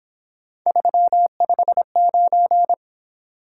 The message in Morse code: 359

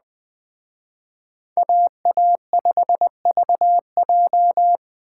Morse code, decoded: AA5VJ

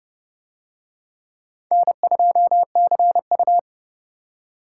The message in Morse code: N2CU

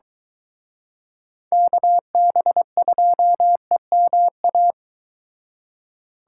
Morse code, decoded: KB2EMA

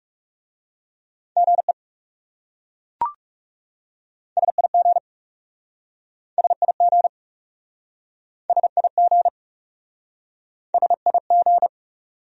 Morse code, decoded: GE SIG SIG SIG SIG

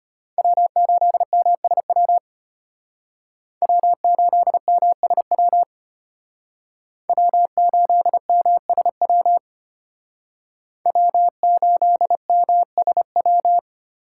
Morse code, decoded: W8MSW W8MSW W8MSW W8MSW

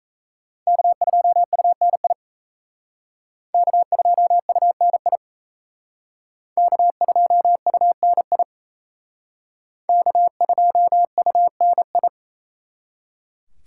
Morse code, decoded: K2UNI K2UNI K2UNI K2UNI

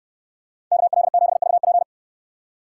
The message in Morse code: KL7VP

